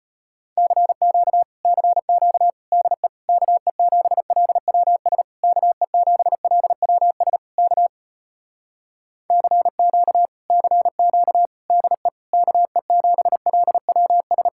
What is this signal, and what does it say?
Morse code, heard as a keyed tone, about 33 wpm, 700 Hz. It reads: CQ CQ DE KE7LWS KE7LWS K CQ CQ DE KE7LWS